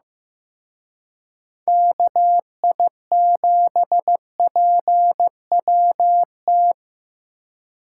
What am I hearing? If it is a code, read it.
Morse code, heard as a keyed tone, about 18 wpm, 700 Hz. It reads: KI7PWT
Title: Speed Racing - USA 2x3 Call Signs 15wpm